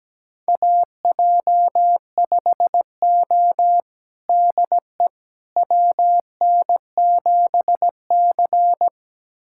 Morse code, heard as a keyed tone, about 17 wpm, 700 Hz. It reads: AJ5O DE WN7C
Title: USA Callsign DE Callsign - 4 Characters - 17wpm